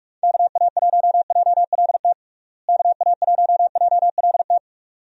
Morse code, heard as a keyed tone, about 41 wpm, 700 Hz. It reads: KA1JLT KA1JLT
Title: USA Call Signs - 6 Characters - Repeated 2x - 45wpm